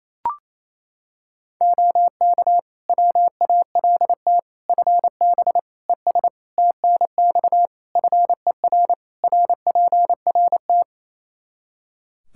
Morse code, heard as a keyed tone, about 28 wpm, 700 Hz. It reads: OK WALT FB ES TNX FER RPRT
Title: Practice QSO - Exchange 2 - Info - 28wpm